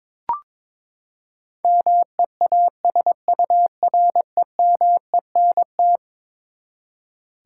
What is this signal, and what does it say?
Morse code, heard as a keyed tone, about 22 wpm, 700 Hz. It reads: MEASUREMENT
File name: practice / 22wpm / Eleven Letter Words - 22wpm